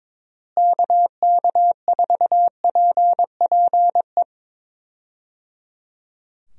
Morse code, heard as a keyed tone, about 22 wpm, 700 Hz. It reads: KK4PPE
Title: USA Call Signs - 6 Characters - Repeated 3x - 22wpm